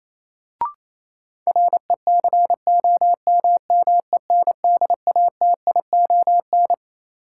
Morse code, heard as a keyed tone, about 28 wpm, 700 Hz. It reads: RECOMMENDATION
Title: Sentences from Top 1400 Words - New Word Drill - 28wpm